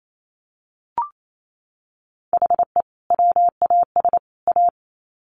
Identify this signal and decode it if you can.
Morse code, heard as a keyed tone, about 28 wpm, 700 Hz. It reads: HE WAS A
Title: Sets of 3 Words 28wpm